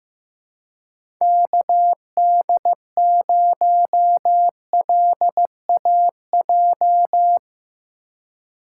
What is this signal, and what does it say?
Morse code, heard as a keyed tone, about 15 wpm, 700 Hz. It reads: KD0LAJ
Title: USA Call Signs - 4x Character Spacing - 15wpm